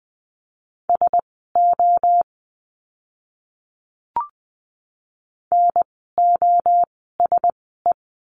Morse code, heard as a keyed tone, about 20 wpm, spaced as for 14 wpm, 700 Hz. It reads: SO NOSE
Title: 061 Lesson 11f - Letter h - Words 20wpm